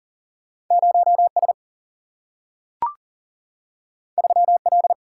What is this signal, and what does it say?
Morse code, heard as a keyed tone, about 40 wpm, 700 Hz. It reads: 0 S 3 L
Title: Single-Letter Single-Number 40wpm